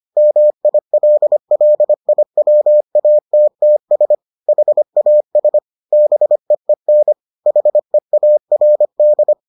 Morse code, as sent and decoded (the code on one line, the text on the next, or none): -- .. .-.. .-.. .. .-- .- - - ... / .... .- ... / -... . -. / .... .- .-. -..
MILLIWATTS HAS BEEN HEARD